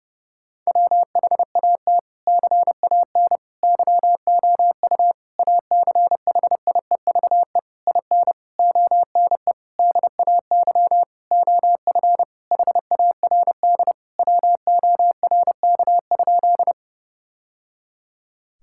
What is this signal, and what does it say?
Morse code, heard as a keyed tone, about 30 wpm, 700 Hz. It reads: WHAT CAN YOU ACHIEVE IN ONE DAY OF HARD WORK?
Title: Sentences from Top 1000 Words - Review Part 6 - 30wpm